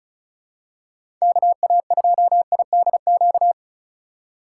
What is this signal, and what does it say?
Morse code, heard as a keyed tone, about 36 wpm, 700 Hz. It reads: KA2IDQ
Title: USA Call Signs - 4x Character Spacing - 35wpm